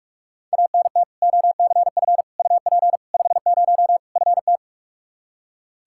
Morse code, heard as a keyed tone, about 46 wpm, 700 Hz. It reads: ANT OCF UP 50 FT
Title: Practice QSO - Exchange 2 - Info - 45wpm